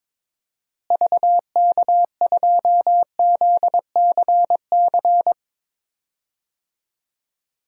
Morse code, heard as a keyed tone, about 22 wpm, 700 Hz. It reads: VK2ZCC
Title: DXCC-100 Call Signs - 6 Characters - Repeated 2x - 22wpm